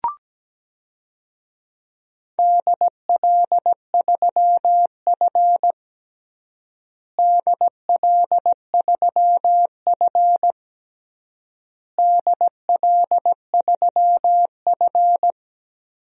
Morse code, heard as a keyed tone, about 17 wpm, 700 Hz. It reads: DL3F DL3F DL3F
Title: DXCC-100 Call Signs - 4 Characters - Repeated 3x - 17wpm